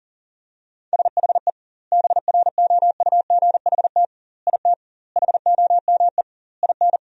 Morse code, heard as a keyed tone, about 40 wpm, 700 Hz. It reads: SHE BROUGHT IT HOME IN
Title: 2x Word-Spacing - Sentences from Top 500 Words - 40wpm